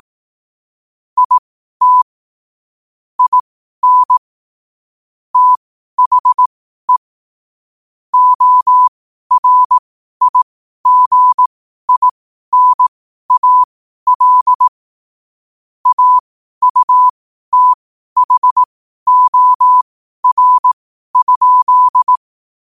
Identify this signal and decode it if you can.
Morse code, heard as a keyed tone, about 18 wpm, spaced as for 12 wpm, 1000 Hz. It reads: IT IN THE ORIGINAL AUTHOR?
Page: Quotes for Wed, 13 Aug 2025 in Morse Code at 12 words per minute.